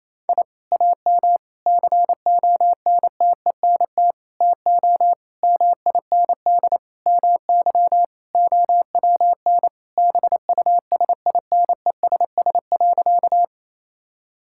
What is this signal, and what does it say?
Morse code, heard as a keyed tone, about 28 wpm, 700 Hz. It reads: I AM CONTENT TO MIND MY OWN BUSINESS.
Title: Sentences from Top 1000 Words - Review Part 6 - 28wpm